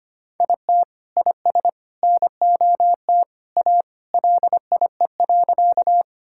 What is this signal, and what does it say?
Morse code, heard as a keyed tone, about 25 wpm, 700 Hz. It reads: IT IS NOT A LIE.
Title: Sentences from Top 1600 Words 25wpm